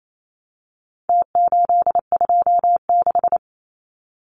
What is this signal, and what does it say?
Morse code, heard as a keyed tone, about 28 wpm, 700 Hz. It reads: T826
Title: Numbers - Contest Style - 4-Digit 28wpm